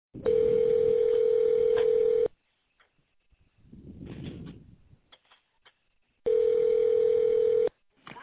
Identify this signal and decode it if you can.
Ringback tone (North America), 440 + 480 Hz.